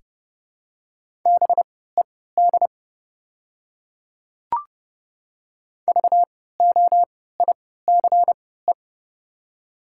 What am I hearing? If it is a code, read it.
Morse code, heard as a keyed tone, about 30 wpm, spaced as for 17 wpm, 700 Hz. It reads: BED VOICE
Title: 3x Character Spacing - Top 500 Words - 30wpm